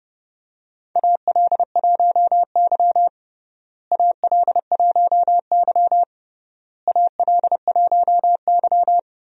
Morse code, heard as a keyed tone, about 32 wpm, 700 Hz. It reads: AL1Y AL1Y AL1Y
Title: USA Call Signs - 4 Characters - Repeated 3x - 30wpm